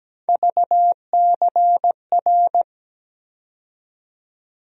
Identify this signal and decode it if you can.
Morse code, heard as a keyed tone, about 17 wpm, 700 Hz. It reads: VCR
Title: USA Call Sign Postfixes - 3 Letters 17wpm